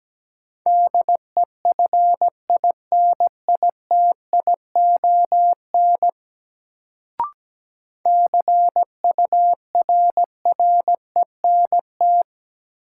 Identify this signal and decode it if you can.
Morse code, heard as a keyed tone, about 17 wpm, 700 Hz. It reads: DEFINITION CURRENT
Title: Sentences from Top 1000 Words - Review Part 4 - New Word Drill - 17wpm